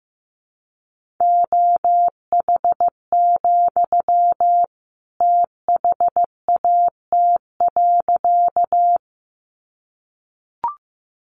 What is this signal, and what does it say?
Morse code, heard as a keyed tone, about 15 wpm, 700 Hz. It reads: OH, THAT.
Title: Sentences from Top 500 Words 15wpm